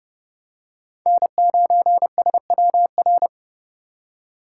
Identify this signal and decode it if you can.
Morse code, heard as a keyed tone, about 30 wpm, 700 Hz. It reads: N9SWR
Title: USA Call Signs - 5 Characters - 6x Character Spacing - 30wpm